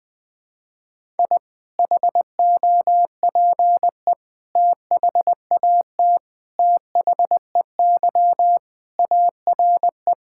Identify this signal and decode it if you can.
Morse code, heard as a keyed tone, about 20 wpm, 700 Hz. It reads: I HOPE THAT THEY ARE
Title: Sentences with 3-Word Phrases - 20wpm